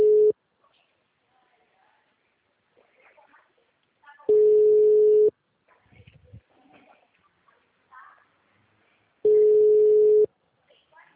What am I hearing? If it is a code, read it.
Ringback tone (Europe, 425 Hz), with speech.